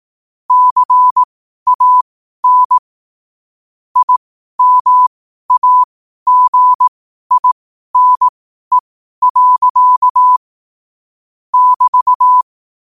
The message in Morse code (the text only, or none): CAN IMAGINE. <BT>